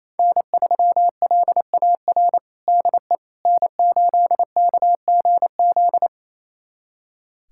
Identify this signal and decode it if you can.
Morse code, heard as a keyed tone, about 27 wpm, 700 Hz. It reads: N3LAR DE N8KGZ
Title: USA Callsign DE Callsign - 5 Characters - 28wpm